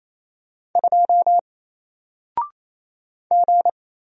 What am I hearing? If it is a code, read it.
Morse code, heard as a keyed tone, about 28 wpm, 700 Hz. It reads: 2 G
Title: Single Letter-Number 28wpm